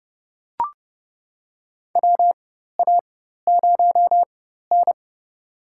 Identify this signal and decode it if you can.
Morse code, heard as a keyed tone, about 31 wpm, spaced as for 14 wpm, 700 Hz. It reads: WA0N
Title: USA Call Signs - 4x Character Spacing - 30wpm